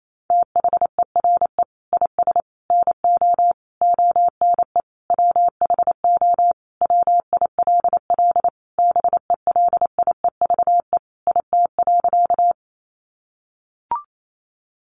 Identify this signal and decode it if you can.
Morse code, heard as a keyed tone, about 28 wpm, 700 Hz. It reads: THERE IS NO ONE WHO WILL BELIEVE IT.
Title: Sentences from Top 1000 Words 28wpm